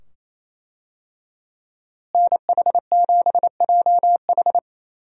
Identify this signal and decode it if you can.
Morse code, heard as a keyed tone, about 28 wpm, 700 Hz. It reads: NH7JH
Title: USA Call Signs - 5 Characters - Repeated 2x - 28wpm